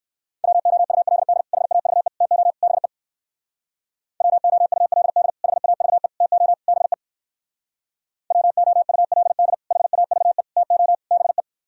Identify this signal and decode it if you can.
Morse code, heard as a keyed tone, about 59 wpm, 700 Hz. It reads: WOULD HAVE TO BE WOULD HAVE TO BE WOULD HAVE TO BE